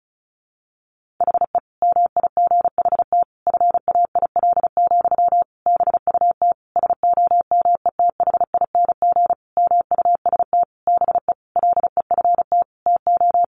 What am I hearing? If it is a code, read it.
Morse code, heard as a keyed tone, about 34 wpm, 700 Hz. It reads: HE MIGHT FAIL, BUT SOMETHING MUST BE LEFT TO
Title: Sentences from Top 1100 Words 35wpm